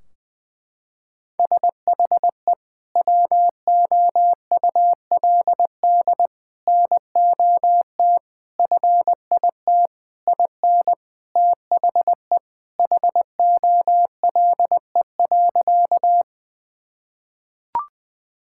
Morse code, heard as a keyed tone, about 21 wpm, 700 Hz. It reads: SHE WOULD NOT FIT IN THE HOLE.
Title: Sentences from Top 1600 Words 20wpm